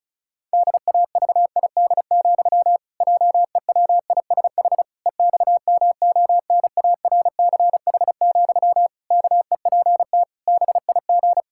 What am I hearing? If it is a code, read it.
Morse code, heard as a keyed tone, about 36 wpm, 700 Hz. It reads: DAVID, JEWISH EXMONARCH, KEPT BIG
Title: Pangram Challenge v2 - 35wpm